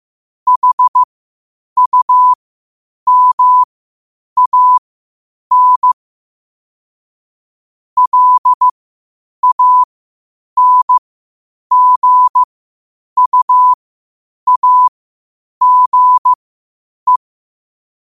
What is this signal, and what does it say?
Morse code, heard as a keyed tone, about 15 wpm, spaced as for 8 wpm, 1000 Hz. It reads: HUMAN LANGUAGE
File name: qotdfiles / 8wpm